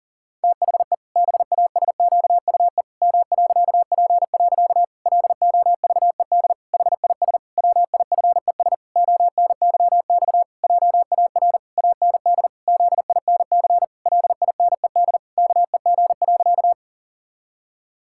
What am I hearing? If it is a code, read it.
Morse code, heard as a keyed tone, about 40 wpm, 700 Hz. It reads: THE BASQUE M.P. LOVED HIS WIFES ONYX JAR AND ZINC LINED KEG.
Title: Pangram Challenge v2 - 40wpm